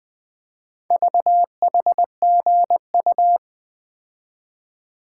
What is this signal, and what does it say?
Morse code, heard as a keyed tone, about 20 wpm, 700 Hz. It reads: VHGU
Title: Four Letter-Number Combinations - 20wpm